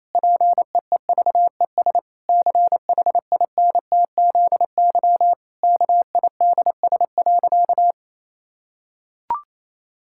Morse code, heard as a keyed tone, about 28 wpm, 700 Hz. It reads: PEEVES CHINTZY KIDS.
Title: Pangram Challenge v2 - 28wpm